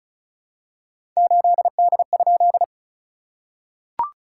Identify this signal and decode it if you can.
Morse code, heard as a keyed tone, about 36 wpm, 690 Hz. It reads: 8D?
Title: SOTA QSO Practice 35wpm